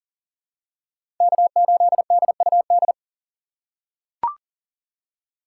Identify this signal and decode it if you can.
Morse code, heard as a keyed tone, about 40 wpm, 700 Hz. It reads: K8DUD
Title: USA Call Signs - 5 Characters - 4x Character Spacing - 40wpm